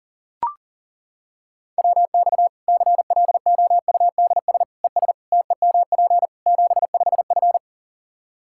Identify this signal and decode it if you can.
Morse code, heard as a keyed tone, about 40 wpm, 700 Hz. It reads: WX CLOUDS ES TEMP 75F
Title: Practice QSO - Exchange 2 - Info - 40wpm